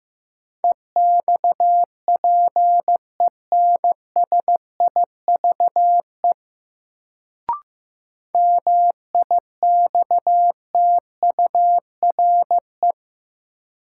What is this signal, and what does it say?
Morse code, heard as a keyed tone, about 15 wpm, 700 Hz. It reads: EXPENSIVE MIXTURE
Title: Sentences from Top 1500 Words - New Word Drill Alt - 15wpm